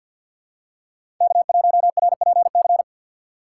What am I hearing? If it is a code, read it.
Morse code, heard as a keyed tone, about 53 wpm, 700 Hz. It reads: K1RPC
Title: USA Call Signs - 5 Characters - 6x Character Spacing - 50wpm